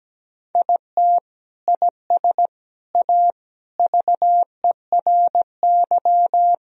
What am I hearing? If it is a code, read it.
Morse code, heard as a keyed tone, about 17 wpm, 700 Hz. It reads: IT IS A VERY
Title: Sentences from Top 1000 Words - Review Part 1 - 17wpm